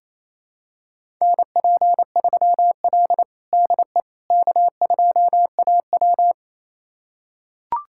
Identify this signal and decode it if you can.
Morse code, heard as a keyed tone, about 28 wpm, 700 Hz. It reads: NP3L DE K2AW
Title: USA Callsign DE Callsign - 4 Characters - 28wpm